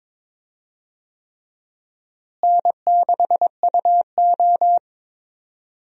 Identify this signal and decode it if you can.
Morse code, heard as a keyed tone, about 22 wpm, 700 Hz. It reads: N6UO
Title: USA Call Signs - 4 Characters - Repeated 3x - 22wpm